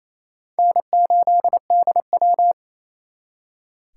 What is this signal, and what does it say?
Morse code, heard as a keyed tone, about 38 wpm, spaced as for 33 wpm, 700 Hz. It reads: N8DW
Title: Speed Racing - USA Call Signs 28wpm